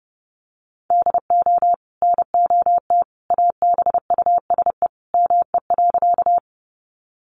Morse code, heard as a keyed tone, about 30 wpm, 700 Hz. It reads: DO NOT ABUSE ME.
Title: Sentences from Top 1300 Words 30wpm